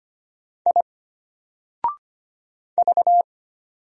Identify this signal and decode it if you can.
Morse code, heard as a keyed tone, about 25 wpm, 700 Hz. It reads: I V
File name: Single-Letter-Number-25wpm